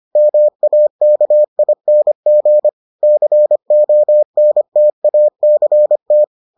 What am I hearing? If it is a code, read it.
Morse code, heard as a keyed tone, about 25 wpm, 600 Hz. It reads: MAKING CONTACT